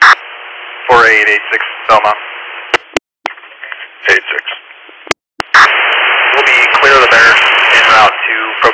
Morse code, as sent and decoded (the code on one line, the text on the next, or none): none